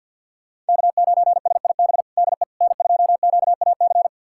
Morse code, heard as a keyed tone, about 51 wpm, 700 Hz. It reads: K9SIB DE N2QAC
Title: USA Callsign DE Callsign - 5 Characters - 50wpm